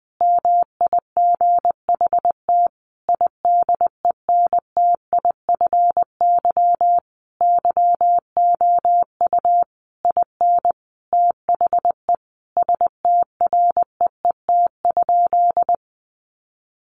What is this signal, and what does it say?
Morse code, heard as a keyed tone, about 20 wpm, 700 Hz. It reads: MIGHT IDENTIFY YOU IN THE STREET?
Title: Sentences from Top 900 Words 20wpm